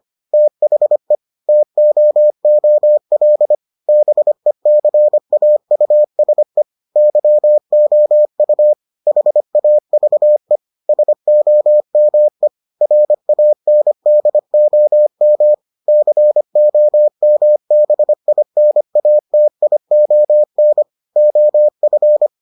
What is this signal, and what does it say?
Morse code, heard as a keyed tone, about 25 wpm, 600 Hz. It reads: THE TOOL BECAUSE YOU HAVE SOME RANDOM COMBINATION OF